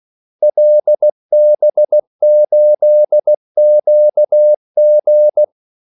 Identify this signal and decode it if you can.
Morse code, heard as a keyed tone, about 16 wpm, 600 Hz. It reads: LB8QG
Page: Kaldesignaler 16 wpm